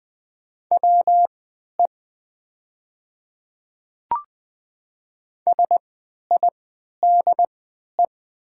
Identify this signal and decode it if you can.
Morse code, heard as a keyed tone, about 20 wpm, spaced as for 11 wpm, 700 Hz. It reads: WE SIDE
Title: 3x Character Spacing - Top 100 Words - 20wpm